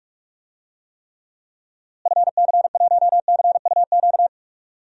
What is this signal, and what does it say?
Morse code, heard as a keyed tone, about 44 wpm, 700 Hz. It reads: FC1CUQ
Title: DXCC-100 Call Signs - 6 Characters - Repeated 3x - 45wpm